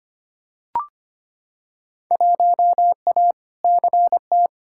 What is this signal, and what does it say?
Morse code, heard as a keyed tone, about 25 wpm, 700 Hz. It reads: 1A CT
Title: Field Day Exchanges Encore - 25wpm